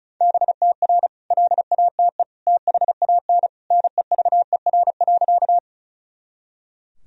Morse code, heard as a keyed tone, about 36 wpm, 700 Hz. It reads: BTR LATE THAN NEVER.
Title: Sentences from Top 400 Words and Abbreviations 35wpm